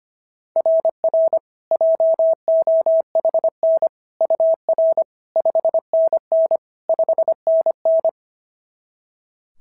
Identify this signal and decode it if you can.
Morse code, heard as a keyed tone, about 25 wpm, 690 Hz. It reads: RR JOHN UR 5NN 5NN
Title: SOTA QSO Practice 25wpm